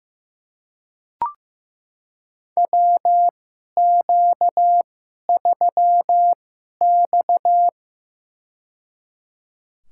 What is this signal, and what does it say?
Morse code, heard as a keyed tone, about 16 wpm, spaced as for 11 wpm, 700 Hz. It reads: WQ3X
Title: USA Call Signs - 4 Characters - 2x Character Spacing - 15wpm